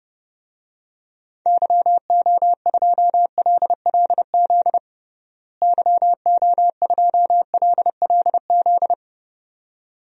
Morse code, heard as a keyed tone, about 30 wpm, 700 Hz. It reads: YO2LLZ YO2LLZ
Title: DXCC-100 Call Signs - 6 Characters - Repeated 2x - 30wpm